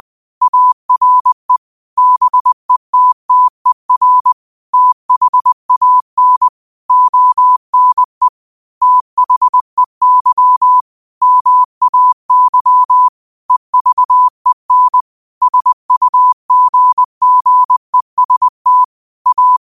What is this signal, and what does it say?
Morse code, heard as a keyed tone, about 20 wpm, 1000 Hz. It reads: ARE BETTER THAN ONE THEY MAY EVEN SUGGEST A
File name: qotdfiles / 20wpm